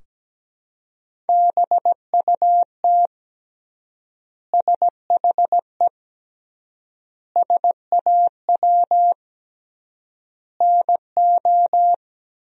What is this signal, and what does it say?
Morse code, heard as a keyed tone, about 17 wpm, 700 Hz. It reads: BUT SHE SAW NO
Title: 3x Word-Spacing - Sentences from Top 300 Words - 17wpm